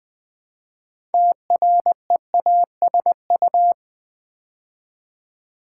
Morse code, heard as a keyed tone, about 20 wpm, 700 Hz. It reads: TREASU
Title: Build A Word - 2 Letters At A Time - 8-Letter Words - 20wpm